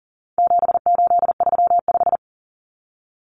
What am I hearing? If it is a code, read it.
Morse code, heard as a keyed tone, about 40 wpm, 700 Hz. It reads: 7835
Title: Numbers - 4-Digit 40wpm